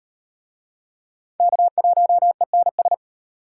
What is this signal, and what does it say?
Morse code, heard as a keyed tone, about 32 wpm, 700 Hz. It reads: K1ENS